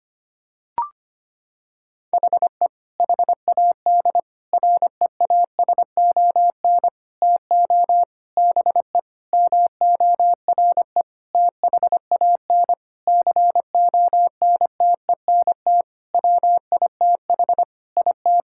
Morse code, heard as a keyed tone, about 25 wpm, 700 Hz. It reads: HE HAD REASON TO BE MORE THAN CONTENT WITH IT
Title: Sentences from Top 1000 Words - Review Part 6 - 25wpm